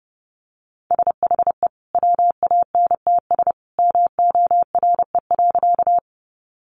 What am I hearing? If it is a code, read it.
Morse code, heard as a keyed tone, about 28 wpm, 700 Hz. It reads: SHE WANTS MORE.
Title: Easy Phrases 30wpm